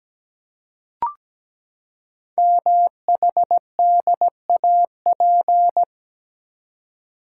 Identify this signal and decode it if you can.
Morse code, heard as a keyed tone, about 17 wpm, 700 Hz. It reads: MHDAP
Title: Code Groups - 5 Letters - 17wpm